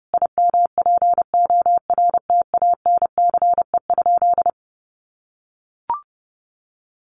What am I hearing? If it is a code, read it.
Morse code, heard as a keyed tone, about 30 wpm, 700 Hz. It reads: IMPORTANCE?
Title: Sentences from Top 1200 Words 30wpm